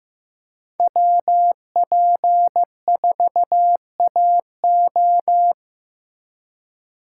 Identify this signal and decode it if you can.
Morse code, heard as a keyed tone, about 15 wpm, 700 Hz. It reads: WP4AO
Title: USA Call Signs - 8x Character Spacing - 15wpm